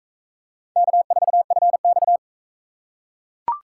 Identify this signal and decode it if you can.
Morse code, heard as a keyed tone, about 42 wpm, 700 Hz. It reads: K4FX